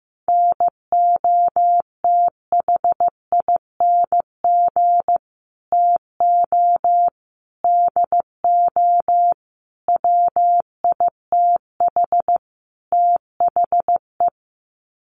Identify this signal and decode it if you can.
Morse code, heard as a keyed tone, about 15 wpm, 700 Hz. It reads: NOTHING TO DO WITH THE
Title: Sets of 5 Words 15wpm